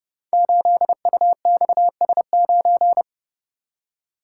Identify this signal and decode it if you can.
Morse code, heard as a keyed tone, about 30 wpm, 700 Hz. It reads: 8UXS9
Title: Code Groups - 5 Letter:Numbers - 30wpm